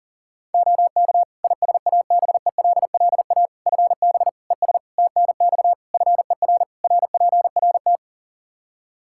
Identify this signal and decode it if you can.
Morse code, heard as a keyed tone, about 40 wpm, 700 Hz. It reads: OK ISABELLA FB ES TNX FER RPRT
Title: Practice QSO - Exchange 2 - Info - 40wpm